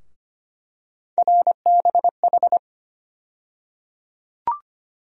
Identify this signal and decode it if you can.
Morse code, heard as a keyed tone, about 25 wpm, 700 Hz. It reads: RBH
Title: USA Call Sign Postfixes - 3 Letters 25wpm